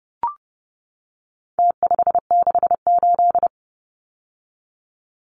Morse code, heard as a keyed tone, about 30 wpm, 700 Hz. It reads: T568
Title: Numbers - Contest Style - 4-Digit 30wpm